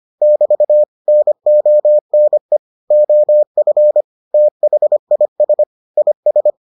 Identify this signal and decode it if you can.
Morse code, heard as a keyed tone, about 25 wpm, 600 Hz. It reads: <BT> NONE OF THIS IS